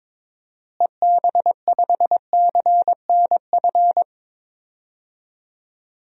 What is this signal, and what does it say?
Morse code, heard as a keyed tone, about 21 wpm, 700 Hz. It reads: EB5CNF
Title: DXCC-100 Call Signs - 6 Characters - Repeated 2x - 22wpm